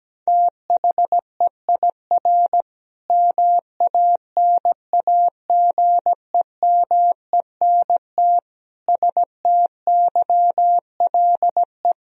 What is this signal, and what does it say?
Morse code, heard as a keyed tone, about 17 wpm, 700 Hz. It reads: THEIR MANAGEMENT STYLE
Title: Sentences from Top 1000 Words - Review Part 4 - 17wpm